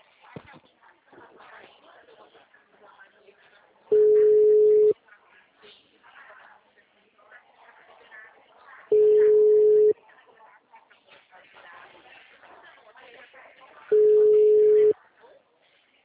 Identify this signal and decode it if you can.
Ringback tone (Europe, 425 Hz), with speech.
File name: out- -2007- - - .0